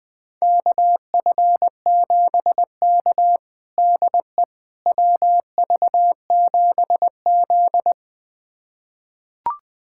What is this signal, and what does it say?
Morse code, heard as a keyed tone, about 20 wpm, 700 Hz. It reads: KF7K DE WV7Z
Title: USA Callsign DE Callsign - 4 Characters - 20wpm